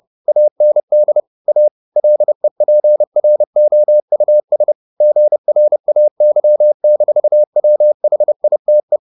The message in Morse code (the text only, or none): AND A LEPROUS GRAY-WHITE